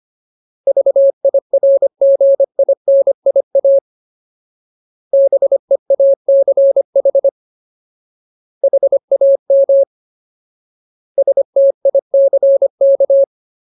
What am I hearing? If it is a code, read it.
Morse code, heard as a keyed tone, about 25 wpm, 550 Hz. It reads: VIRGINIA BEACH HAM STICK
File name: Ultimate Word List / 25WPM Ultimate